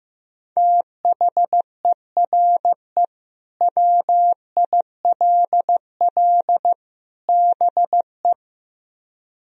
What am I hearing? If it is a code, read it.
Morse code, heard as a keyed tone, about 15 wpm, 700 Hz. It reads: THERE WILL BE
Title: Speed-Racing - Sets of 3 Words 15wpm